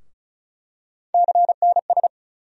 Morse code, heard as a keyed tone, about 36 wpm, 700 Hz. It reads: CNS
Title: USA Call Sign Postfixes - 3 Letters 35wpm